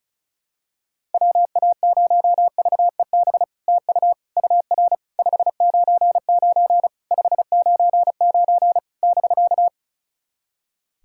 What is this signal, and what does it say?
Morse code, heard as a keyed tone, about 36 wpm, 690 Hz. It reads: WA0VEB TU UR 599 599 <BK>
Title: SOTA QSO Practice 35wpm